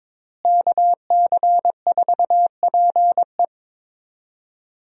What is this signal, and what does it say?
Morse code, heard as a keyed tone, about 22 wpm, 700 Hz. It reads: KC4PE
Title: USA Call Signs - 5 Characters - Repeated 2x - 22wpm